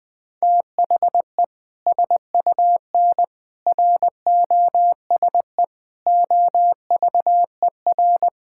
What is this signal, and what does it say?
Morse code, heard as a keyed tone, about 20 wpm, 700 Hz. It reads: THE SUN ROSE OVER
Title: Sentences with 2-Word Phrases - Spoken Beginning - 20wpm